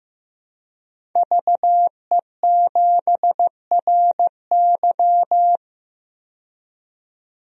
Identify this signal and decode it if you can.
Morse code, heard as a keyed tone, about 15 wpm, 700 Hz. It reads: VE7RY
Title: DXCC-100 Call Signs - 5 Characters - 15wpm